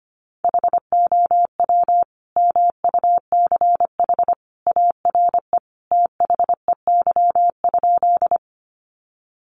Morse code, heard as a keyed tone, about 25 wpm, 700 Hz. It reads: HOW MUCH ARE THEY?
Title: Sentences from Top 200 Words 25wpm